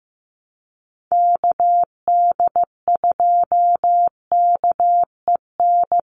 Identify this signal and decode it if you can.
Morse code, heard as a keyed tone, about 15 wpm, 700 Hz. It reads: KD2KEN
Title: USA Call Signs 15wpm